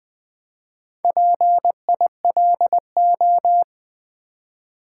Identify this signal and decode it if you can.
Morse code, heard as a keyed tone, about 20 wpm, 700 Hz. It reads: PILO
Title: Build A Word - 1 Letter At A Time - 5-Letter Words - 20wpm